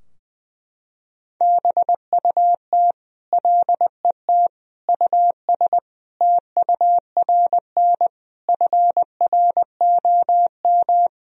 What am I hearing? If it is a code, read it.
Morse code, heard as a keyed tone, about 21 wpm, 700 Hz. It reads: BUT LET US TURN FROM
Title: Sentences from Top 1000 Words - Review Part 4 - 20wpm